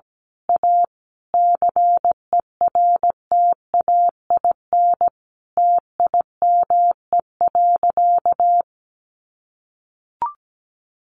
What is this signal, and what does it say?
Morse code, heard as a keyed tone, about 17 wpm, 700 Hz. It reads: A CERTAIN TIME.
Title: Sentences from Top 1000 Words 17wpm